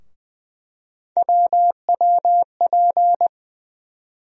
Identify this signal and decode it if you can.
Morse code, heard as a keyed tone, about 20 wpm, 700 Hz. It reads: WWP
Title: USA Call Sign Postfixes - 3 Letters 20wpm